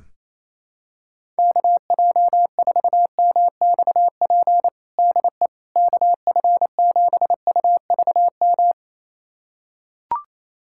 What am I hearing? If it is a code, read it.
Morse code, heard as a keyed tone, about 27 wpm, 700 Hz. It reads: KJ4MXP DE KF7UVM
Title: USA Callsign DE Callsign 28wpm